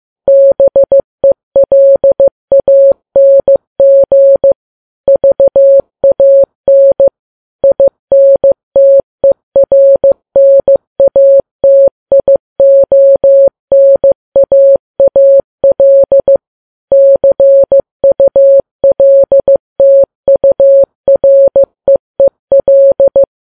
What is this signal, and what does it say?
Morse code, heard as a keyed tone, about 15 wpm, 550 Hz. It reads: BELANG VAN INTERNATIONAAL CULTUREEL
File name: Files Web tls / Morse on9fv / morse15kps